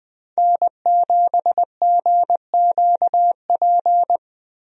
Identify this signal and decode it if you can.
Morse code, heard as a keyed tone, about 21 wpm, 700 Hz. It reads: N7GQP
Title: USA Call Signs - 5 Characters - Repeated 3x - 20wpm